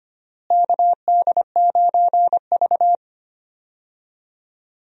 Morse code, heard as a keyed tone, about 25 wpm, 700 Hz. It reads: KD9V
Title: USA Call Signs - 4 Characters - Repeated 2x - 25wpm